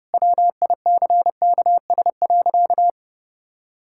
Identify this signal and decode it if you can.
Morse code, heard as a keyed tone, about 28 wpm, 700 Hz. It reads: WICKS.
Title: Pangram Challenge v2 - 30wpm